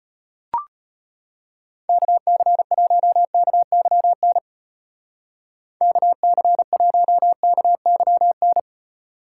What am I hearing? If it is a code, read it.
Morse code, heard as a keyed tone, about 32 wpm, 700 Hz. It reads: KC1KYN KC1KYN